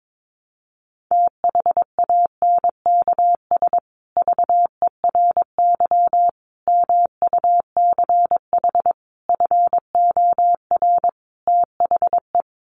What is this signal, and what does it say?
Morse code, heard as a keyed tone, about 22 wpm, 700 Hz. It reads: THANKS VERY MUCH FOR THE
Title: Sets of 5 Words Encore2 22wpm